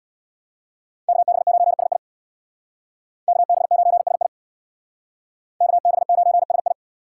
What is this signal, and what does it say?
Morse code, heard as a keyed tone, about 63 wpm, 700 Hz. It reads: KB9SI KB9SI KB9SI